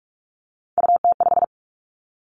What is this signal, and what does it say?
Morse code, heard as a keyed tone, about 46 wpm, 700 Hz. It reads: UT5
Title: Callsign Prefixes for Top 25 DXCC - 45wpm